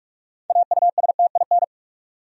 Morse code, heard as a keyed tone, about 44 wpm, 700 Hz. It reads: AUSTIN
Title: USA Common Names - 45wpm